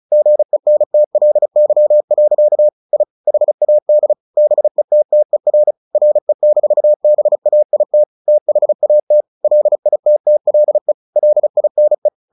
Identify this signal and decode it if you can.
Morse code, heard as a keyed tone, about 35 wpm, 600 Hz. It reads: GENTLY. I HAD BETTER RE-BAIT THAT LITTLE LINE
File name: the old man and the sea / the old man and the sea 1 35wpm